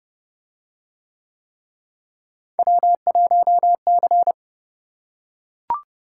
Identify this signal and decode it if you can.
Morse code, heard as a keyed tone, about 30 wpm, 700 Hz. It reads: W1C